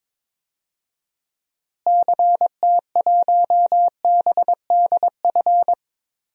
Morse code, heard as a keyed tone, about 22 wpm, 700 Hz. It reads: CT1BDF
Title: DXCC-100 Call Signs - 6 Characters - Repeated 3x - 22wpm